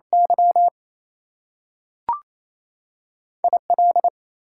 Morse code, heard as a keyed tone, about 28 wpm, 700 Hz. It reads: Y IL
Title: US States and Canadian Provinces 28wpm